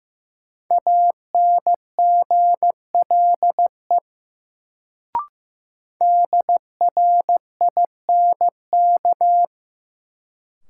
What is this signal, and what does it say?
Morse code, heard as a keyed tone, about 15 wpm, 700 Hz. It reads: ANGLE DRINK
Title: Sentences from Top 1500 Words - New Word Drill Alt - 15wpm